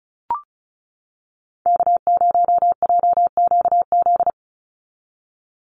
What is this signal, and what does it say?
Morse code, heard as a keyed tone, about 34 wpm, 700 Hz. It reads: K0JQZ
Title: NA SOTA Activator Callsigns 35wpm